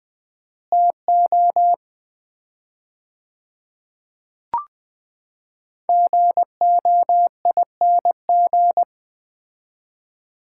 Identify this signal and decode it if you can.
Morse code, heard as a keyed tone, about 20 wpm, 700 Hz. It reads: TO GOING
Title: 4x Word-Spacing - Sets of 5 Words - 20wpm